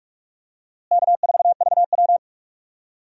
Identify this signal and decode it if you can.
Morse code, heard as a keyed tone, about 47 wpm, 700 Hz. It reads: K4VW
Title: USA Call Signs - 4 Characters - Repeated 2x - 45wpm